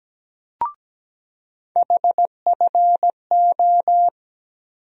Morse code, heard as a keyed tone, about 17 wpm, 700 Hz. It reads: HFO
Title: USA Call Sign Postfixes - 3 Letters 17wpm